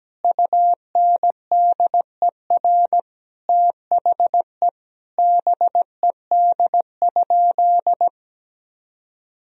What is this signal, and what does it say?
Morse code, heard as a keyed tone, about 17 wpm, 700 Hz. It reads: UNDER THE BED?
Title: Sentences from Top 1000 Words - Review Part 1 - 17wpm